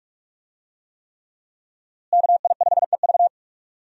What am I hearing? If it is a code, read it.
Morse code, heard as a keyed tone, about 46 wpm, 700 Hz. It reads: KI5EV
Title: USA Call Signs - 5 Characters - Repeated 3x - 45wpm